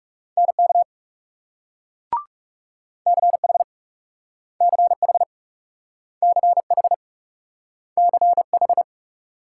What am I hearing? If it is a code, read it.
Morse code, heard as a keyed tone, about 42 wpm, 700 Hz. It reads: NK CH CH CH CH